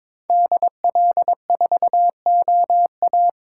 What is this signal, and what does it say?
Morse code, heard as a keyed tone, about 26 wpm, 700 Hz. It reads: DL4OA
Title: Speed Racing - USA 2x2 Call Signs 22wpm